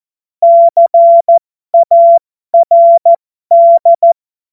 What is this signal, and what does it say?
Morse code, heard as a keyed tone, about 14 wpm, spaced as for 12 wpm, 680 Hz. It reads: CARD
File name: Morsekurs / Witze